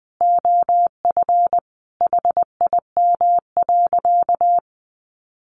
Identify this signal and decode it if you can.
Morse code, heard as a keyed tone, about 20 wpm, 700 Hz. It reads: OF HIM.
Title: Sentences from Top 1100 Words 20wpm